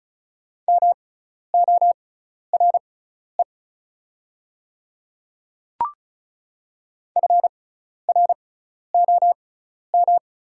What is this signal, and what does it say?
Morse code, heard as a keyed tone, about 36 wpm, spaced as for 12 wpm, 700 Hz. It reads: MORE FROM
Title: 6x Character Spacing - Top 100 Words - 35wpm